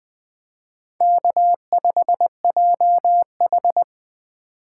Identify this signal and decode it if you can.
Morse code, heard as a keyed tone, about 19 wpm, 700 Hz. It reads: K5JH